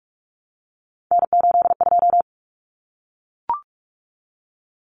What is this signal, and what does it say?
Morse code, heard as a keyed tone, about 44 wpm, 700 Hz. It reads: N82
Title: Common Cut Numbers - 3-Digit 45wpm